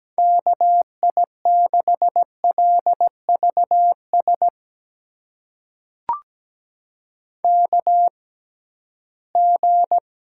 Morse code, heard as a keyed tone, about 17 wpm, spaced as for 13 wpm, 700 Hz. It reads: KI6LVS KG
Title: USA Call Signs - 6 Characters - 6x Character Spacing - 17wpm